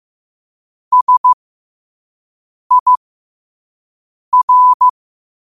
Morse code, heard as a keyed tone, about 15 wpm, spaced as for 5 wpm, 1000 Hz. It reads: SIR
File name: qotdfiles / 5wpm